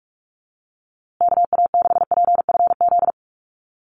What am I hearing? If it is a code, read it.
Morse code, heard as a keyed tone, about 44 wpm, 700 Hz. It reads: KA6PFZ
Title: USA Call Signs Repeated 2x 45wpm